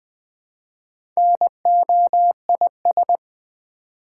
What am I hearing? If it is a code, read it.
Morse code, heard as a keyed tone, about 20 wpm, 700 Hz. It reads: NOIS
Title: Build A Word - 1 Letter At A Time - 5-Letter Words - 20wpm